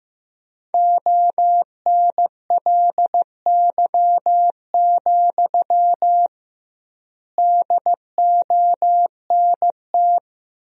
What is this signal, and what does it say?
Morse code, heard as a keyed tone, about 15 wpm, 700 Hz. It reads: ONLY, DONT
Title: 2x Word-Spacing - Sentences from Top 300 Words - 15wpm